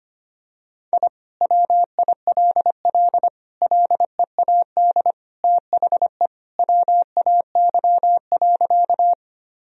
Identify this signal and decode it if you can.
Morse code, heard as a keyed tone, about 25 wpm, 700 Hz. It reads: I WILL LEAD THE WAY.
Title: Sentences from Top 1000 Words - Review Part 2 - 25wpm